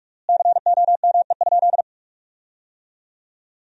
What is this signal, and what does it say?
Morse code, heard as a keyed tone, about 46 wpm, 700 Hz. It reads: COME?
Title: 4x Word-Spacing - Sentences from Top 200 Words - 45wpm